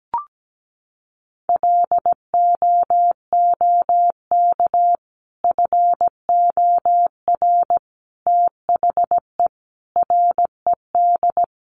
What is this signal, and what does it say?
Morse code, heard as a keyed tone, about 17 wpm, 700 Hz. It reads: LOOK FOR THE RED
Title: A Sound Of Thunder 17wpm